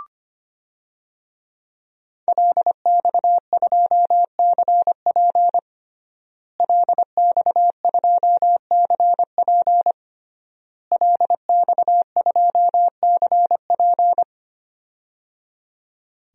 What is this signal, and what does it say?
Morse code, heard as a keyed tone, about 25 wpm, 700 Hz. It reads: LX2CP LX2CP LX2CP
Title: DXCC-100 Call Signs - 5 Characters - Repeated 3x - 25wpm